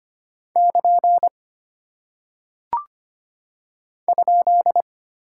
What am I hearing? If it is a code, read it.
Morse code, heard as a keyed tone, about 25 wpm, 700 Hz. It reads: <KN> ?
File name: Single Punctuation-Prosign - 25wpm